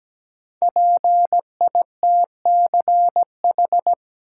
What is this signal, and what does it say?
Morse code, heard as a keyed tone, about 17 wpm, 700 Hz. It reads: PITCH
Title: Sentences from Top 1300 Words - New Word Drill - 17wpm